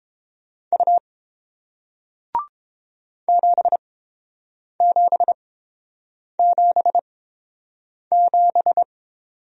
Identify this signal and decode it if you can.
Morse code, heard as a keyed tone, about 29 wpm, spaced as for 7 wpm, 700 Hz. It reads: U 7777